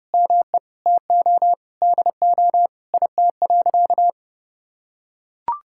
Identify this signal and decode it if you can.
Morse code, heard as a keyed tone, about 30 wpm, 700 Hz. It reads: ME TO DO IT.
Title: Sentences from Top 1000 Words - Review Part 5 - 30wpm